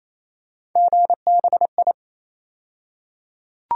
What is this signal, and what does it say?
Morse code, heard as a keyed tone, about 27 wpm, 700 Hz. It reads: GBI
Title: USA Call Sign Postfixes - 3 Letters 28wpm